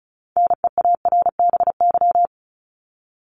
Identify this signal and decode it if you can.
Morse code, heard as a keyed tone, about 34 wpm, 700 Hz. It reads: NEARBY
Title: Top 1501-1750 Words 35wpm